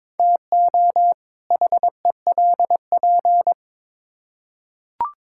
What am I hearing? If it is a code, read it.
Morse code, heard as a keyed tone, about 21 wpm, 700 Hz. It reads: TO HELP
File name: Build A Phrase - 2-Word Phrases - 22wpm